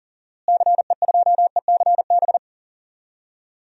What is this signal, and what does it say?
Morse code, heard as a keyed tone, about 40 wpm, 700 Hz. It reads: CE2ECB
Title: DXCC-100 Call Signs - 6 Characters - Repeated 3x - 40wpm